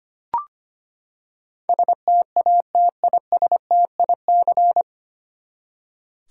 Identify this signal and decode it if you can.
Morse code, heard as a keyed tone, about 25 wpm, 700 Hz. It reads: STATISTIC
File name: Nine Letter Words - 25wpm